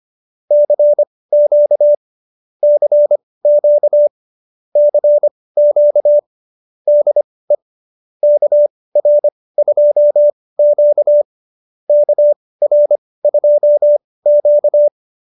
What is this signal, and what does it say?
Morse code, heard as a keyed tone, about 25 wpm, spaced as for 18 wpm, 590 Hz. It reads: CQ CQ CQ DE KR2Q KR2Q